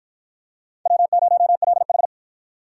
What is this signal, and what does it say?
Morse code, heard as a keyed tone, about 52 wpm, 700 Hz. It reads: W0LH